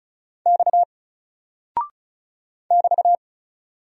Morse code, heard as a keyed tone, about 36 wpm, 700 Hz. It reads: X <BT>
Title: Single Character (Letter:Number:Punctuation:Prosign) 35wpm